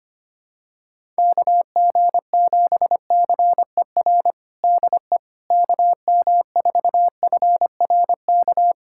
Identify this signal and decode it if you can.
Morse code, heard as a keyed tone, about 25 wpm, 700 Hz. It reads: KG7CER DE KM4FRK
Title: USA Callsign DE Callsign 25wpm